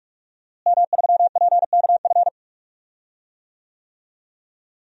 Morse code, heard as a keyed tone, about 46 wpm, 700 Hz. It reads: M3PKF
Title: DXCC-100 Call Signs - 5 Characters - Repeated 3x - 45wpm